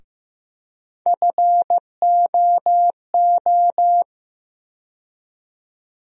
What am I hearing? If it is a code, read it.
Morse code, heard as a keyed tone, about 15 wpm, 700 Hz. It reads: FOO
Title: USA Call Sign Postfixes - 3 Letters 15wpm